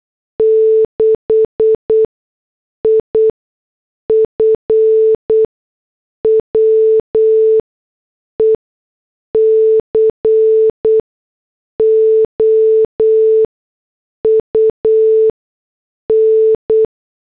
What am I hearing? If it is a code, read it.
Morse code, heard as a keyed tone, about 8 wpm, spaced as for 6 wpm, 440 Hz. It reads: 6 IF WE COUN
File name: morse / headlines